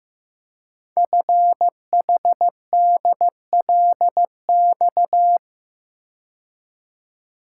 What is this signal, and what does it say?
Morse code, heard as a keyed tone, about 16 wpm, 700 Hz. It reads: FHDLX
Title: Code Groups - 5 Letter:Numbers - 15wpm